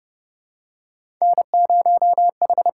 Morse code, heard as a keyed tone, about 30 wpm, 700 Hz. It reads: N0H
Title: USA Call Signs - 5 Characters - 8x Character Spacing - 30wpm